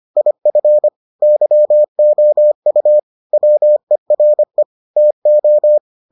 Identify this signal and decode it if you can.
Morse code, heard as a keyed tone, about 25 wpm, 600 Hz. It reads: IF YOU WERE TO